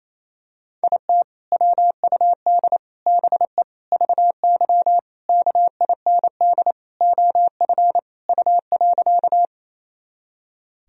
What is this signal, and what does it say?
Morse code, heard as a keyed tone, about 28 wpm, 700 Hz. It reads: IT WUD BE VY KIND OF U.
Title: Sentences from Top 200 Words and Abbreviations 28wpm